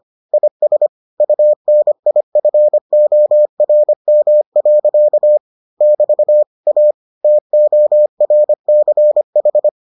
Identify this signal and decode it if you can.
Morse code, heard as a keyed tone, about 25 wpm, 600 Hz. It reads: IS UNIFORM. <BT> A TORCH